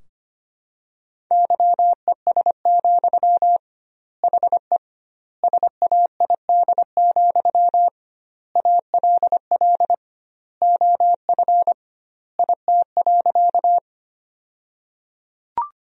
Morse code, heard as a keyed tone, about 25 wpm, 700 Hz. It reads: YES, HE SAID, ALL OF IT.
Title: 2x Word-Spacing - Sentences from Top 500 Words - 25wpm